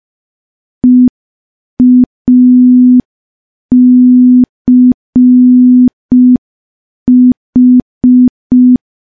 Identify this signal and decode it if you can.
Morse code, heard as a keyed tone, about 5 wpm, 250 Hz. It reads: EACH
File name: morse / morse-training-05WPM